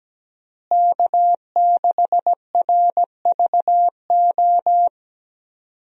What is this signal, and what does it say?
Morse code, heard as a keyed tone, about 17 wpm, 700 Hz. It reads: K6RVO
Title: USA Call Signs - 5 Characters - Repeated 3x - 17wpm